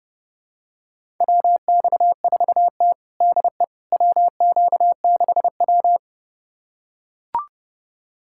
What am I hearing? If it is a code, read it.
Morse code, heard as a keyed tone, about 30 wpm, 700 Hz. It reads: WX4T DE WQ6W
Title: USA Callsign DE Callsign - 4 Characters - 30wpm